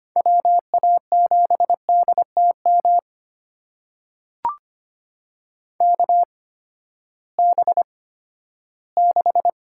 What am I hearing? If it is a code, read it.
Morse code, heard as a keyed tone, about 25 wpm, spaced as for 14 wpm, 700 Hz. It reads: WA7DTM KB6
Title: USA Call Signs - 6 Characters - 8x Character Spacing - 25wpm